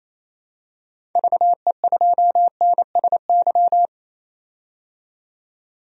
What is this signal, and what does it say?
Morse code, heard as a keyed tone, about 28 wpm, 700 Hz. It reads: VE2NSY
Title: DXCC-100 Call Signs - 6 Characters - 28wpm